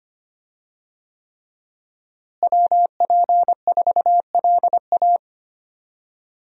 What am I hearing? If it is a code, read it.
Morse code, heard as a keyed tone, about 25 wpm, 700 Hz. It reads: WP4LA
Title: USA Call Signs - 5 Characters - Repeated 3x - 25wpm